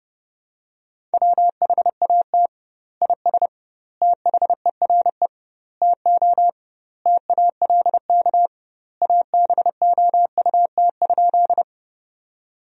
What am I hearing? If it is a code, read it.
Morse code, heard as a keyed tone, about 30 wpm, 700 Hz. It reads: WHAT IS THERE TO TALK ABOUT?
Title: 2x Word-Spacing - Sentences from Top 400 Words - 30wpm